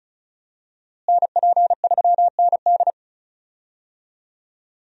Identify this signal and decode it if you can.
Morse code, heard as a keyed tone, about 34 wpm, 700 Hz. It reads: NP3ND
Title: USA Call Signs - 5 Characters - Repeated 3x - 35wpm